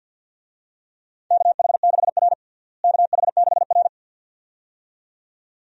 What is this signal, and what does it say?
Morse code, heard as a keyed tone, about 49 wpm, 700 Hz. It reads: KH6R KH6R
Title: DXCC-100 Call Signs - 4 Characters - Repeated 2x - 50wpm